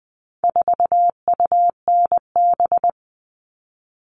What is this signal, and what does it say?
Morse code, heard as a keyed tone, about 20 wpm, 700 Hz. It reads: 4UNB
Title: All Cut Numbers - 4-Digit 20wpm